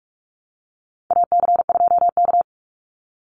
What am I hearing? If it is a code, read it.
Morse code, heard as a keyed tone, about 44 wpm, 700 Hz. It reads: AC2K
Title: Call Signs Repeated 2x - 45wpm